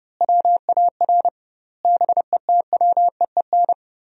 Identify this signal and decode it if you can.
Morse code, heard as a keyed tone, about 32 wpm, 700 Hz. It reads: WAR BETWEEN
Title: 2x Word-Spacing - Sentences from Top 400 Words - 30wpm